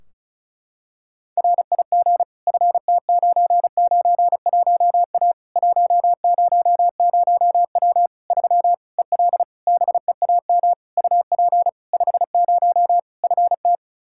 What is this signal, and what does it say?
Morse code, heard as a keyed tone, about 36 wpm, 700 Hz. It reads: RIG FT991A 100W 3 EL BEAM UP 50 FT
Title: K1USN SST QSO Practice 35wpm